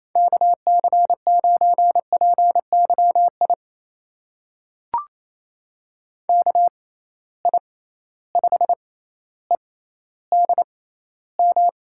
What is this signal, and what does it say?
Morse code, heard as a keyed tone, about 28 wpm, spaced as for 14 wpm, 700 Hz. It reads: KC9PYI KI5EDM
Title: USA Call Signs - 6 Characters - 6x Character Spacing - 28wpm